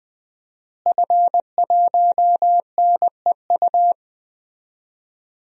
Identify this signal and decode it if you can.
Morse code, heard as a keyed tone, about 20 wpm, 700 Hz. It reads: F1NEU
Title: DXCC-100 Call Signs - 5 Characters - Repeated 2x - 20wpm